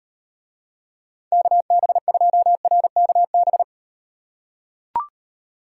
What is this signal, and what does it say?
Morse code, heard as a keyed tone, about 38 wpm, 700 Hz. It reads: KB2RKB